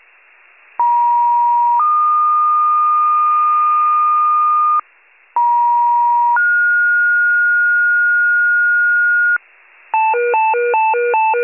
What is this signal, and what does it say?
Two-tone page, heard as a one-second tone, then a three-second tone, about 950 then 1220 Hz, about 950 then 1430 Hz, with noise. It is heard from 0.8 to 9.4 s.
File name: pager / 83 RESCUE OFC / wcalert